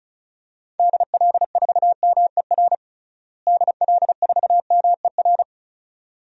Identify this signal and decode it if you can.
Morse code, heard as a keyed tone, about 36 wpm, 700 Hz. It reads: DL4MER DL4MER
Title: DXCC-100 Call Signs - 6 Characters - Repeated 2x - 35wpm